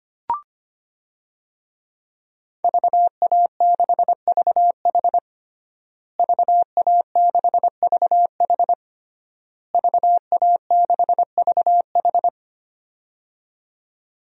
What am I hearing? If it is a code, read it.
Morse code, heard as a keyed tone, about 25 wpm, 700 Hz. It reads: VA6VH VA6VH VA6VH
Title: DXCC-100 Call Signs - 5 Characters - Repeated 3x - 25wpm